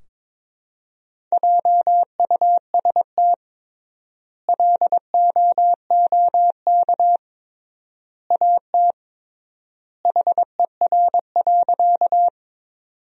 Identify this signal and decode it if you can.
Morse code, heard as a keyed tone, about 22 wpm, 700 Hz. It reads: JUST LOOK AT HER.
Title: 3x Word-Spacing - Sentences from Top 200 Words - 22wpm